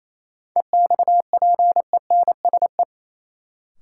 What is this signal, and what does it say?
Morse code, heard as a keyed tone, about 28 wpm, 700 Hz. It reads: EXPENSE
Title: Sets of 4 Words - New Word Drill - 28wpm